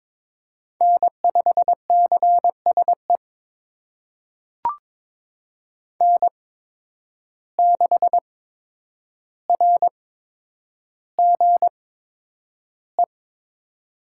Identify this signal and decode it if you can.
Morse code, heard as a keyed tone, about 22 wpm, spaced as for 9 wpm, 700 Hz. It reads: N5CSE N6RGE
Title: USA Call Signs - 5 Characters - 8x Character Spacing - 22wpm